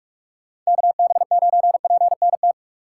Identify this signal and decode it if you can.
Morse code, heard as a keyed tone, about 46 wpm, 700 Hz. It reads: KB9PNT
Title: USA Call Signs - 6 Characters - 2x Character Spacing - 45wpm